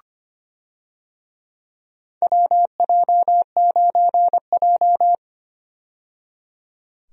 Morse code, heard as a keyed tone, about 25 wpm, 700 Hz. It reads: WJ9J
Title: USA Call Signs - 4 Characters - 25wpm